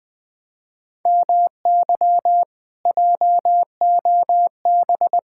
Morse code, heard as a keyed tone, about 20 wpm, 700 Hz. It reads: MY JOB
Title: Speed-Racing Common QSO Phrases - 17wpm